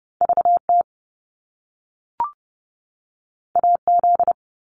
Morse code, heard as a keyed tone, about 30 wpm, 700 Hz. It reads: VT AZ
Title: US States 30wpm